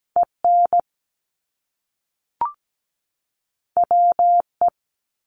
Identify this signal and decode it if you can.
Morse code, heard as a keyed tone, about 17 wpm, 700 Hz. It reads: EN WE
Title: Top Two Letter Combinations in Words 17wpm